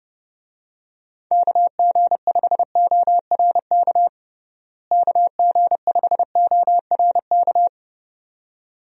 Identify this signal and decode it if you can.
Morse code, heard as a keyed tone, about 30 wpm, 700 Hz. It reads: KG5ORK KG5ORK
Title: USA Call Signs - 6 Characters - Repeated 2x - 30wpm